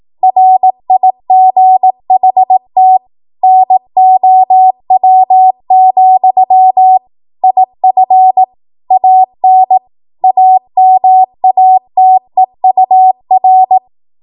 Morse code, heard as a keyed tone, about 18 wpm, 750 Hz. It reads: RIGHT NOW, IF AN AMATEUR
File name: Morse / 18WPM